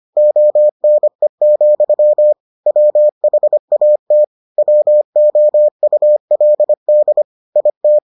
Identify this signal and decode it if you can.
Morse code, heard as a keyed tone, about 25 wpm, 600 Hz. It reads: ONE, WHAT WOULD IT